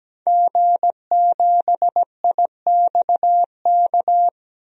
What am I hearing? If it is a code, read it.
Morse code, heard as a keyed tone, about 17 wpm, 700 Hz. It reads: G7IXK
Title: DXCC-100 Call Signs - 5 Characters - 17wpm